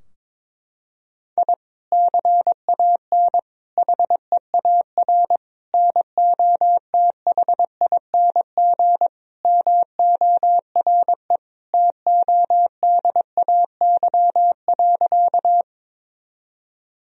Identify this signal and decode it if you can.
Morse code, heard as a keyed tone, about 21 wpm, 700 Hz. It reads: I CAN HEAR NOTHING MORE TODAY.
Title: Sentences from Top 1000 Words - Review Part 4 - 22wpm